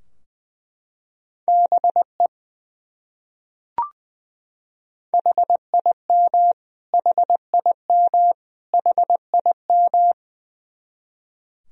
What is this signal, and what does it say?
Morse code, heard as a keyed tone, about 20 wpm, 700 Hz. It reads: BE HIM HIM HIM
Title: Top 100 Words Repeated 20wpm